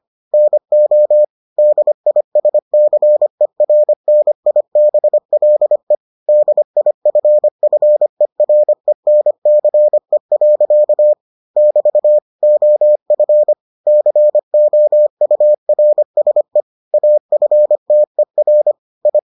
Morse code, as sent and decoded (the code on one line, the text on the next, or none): -. --- / -.. .. ... -.-. .-. -. .. -... .-.. / -.. .. ..-. ..-. .-. -. -.-. .-.-.- / -...- / --- ..-. / -.-. --- ..- .-. ... / .- ..-. - . .-. / ..
NO DISCERNIBLE DIFFERENCE. <BT> OF COURSE AFTER I